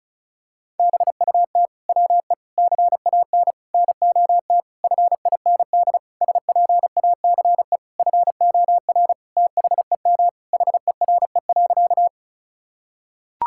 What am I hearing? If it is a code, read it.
Morse code, heard as a keyed tone, about 36 wpm, 700 Hz. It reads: BUT WE CAN NOT FIND SPACE FOR THEM HERE.
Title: Sentences from Top 1000 Words - Review Part 5 - 35wpm